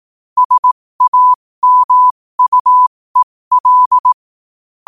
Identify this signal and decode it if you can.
Morse code, heard as a keyed tone, about 18 wpm, spaced as for 15 wpm, 1000 Hz. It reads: SAMUEL
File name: qotdfiles / 15wpm